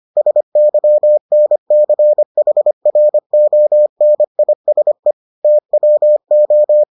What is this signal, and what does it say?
Morse code, heard as a keyed tone, about 25 wpm, 600 Hz. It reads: SYNCHRONISE TWO